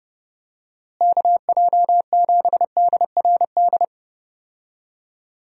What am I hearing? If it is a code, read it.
Morse code, heard as a keyed tone, about 30 wpm, 700 Hz. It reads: KJ7DRD
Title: USA Call Signs - 6 Characters - Repeated 2x - 30wpm